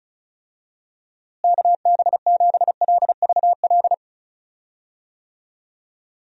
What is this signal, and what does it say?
Morse code, heard as a keyed tone, about 36 wpm, 700 Hz. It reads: KB7LVL
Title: USA Call Signs - 6 Characters - Repeated 3x - 35wpm